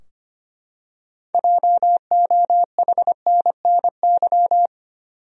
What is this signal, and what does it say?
Morse code, heard as a keyed tone, about 25 wpm, 700 Hz. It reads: JOHNNY
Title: USA Common Names - 25wpm